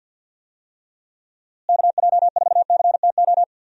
Morse code, heard as a keyed tone, about 52 wpm, 700 Hz. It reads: KJ4CTO
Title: USA Call Signs - 6 Characters - Repeated 2x - 50wpm